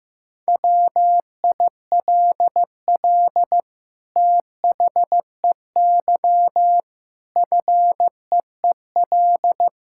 Morse code, heard as a keyed tone, about 15 wpm, 700 Hz. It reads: WILL THEY FEEL
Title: Sentences from Top 1000 Words - Review Part 5 - 15wpm